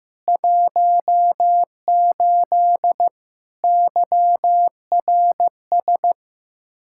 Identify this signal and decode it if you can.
Morse code, heard as a keyed tone, about 15 wpm, 700 Hz. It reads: 18 YRS
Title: Practice QSO - Exchange 3 - More Info - 15wpm